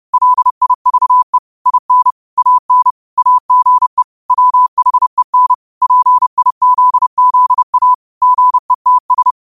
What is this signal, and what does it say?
Morse code, heard as a keyed tone, about 30 wpm, 1000 Hz. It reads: LIVE IN AN AGE WHEN PIZZA GETS
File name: qotdfiles / 30wpm